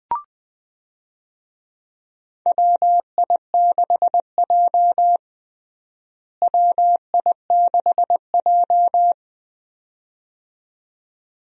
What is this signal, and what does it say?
Morse code, heard as a keyed tone, about 20 wpm, 700 Hz. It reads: WI6J WI6J
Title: USA Call Signs - 4 Characters - Repeated 2x - 20wpm